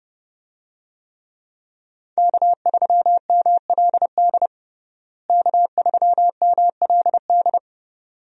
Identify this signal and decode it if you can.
Morse code, heard as a keyed tone, about 30 wpm, 700 Hz. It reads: K3MLD K3MLD
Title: USA Call Signs - 5 Characters - Repeated 2x - 30wpm